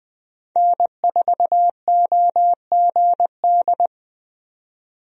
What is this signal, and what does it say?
Morse code, heard as a keyed tone, about 20 wpm, 700 Hz. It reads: N4OGD
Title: USA Call Signs - 5 Characters - 20wpm